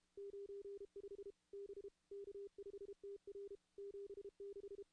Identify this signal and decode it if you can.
Morse code, heard as a keyed tone, about 32 wpm, 400 Hz. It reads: 95 B K5TR 76
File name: contests / CW-SS-k5tr- -2100